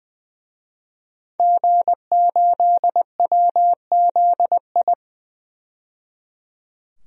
Morse code, heard as a keyed tone, about 20 wpm, 700 Hz. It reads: G8WZI
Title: DXCC-100 Call Signs - 5 Characters - 20wpm